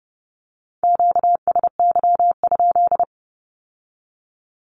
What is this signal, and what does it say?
Morse code, heard as a keyed tone, about 30 wpm, 700 Hz. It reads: QSY?
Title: Q-Codes with Spoken Meaning - 30wpm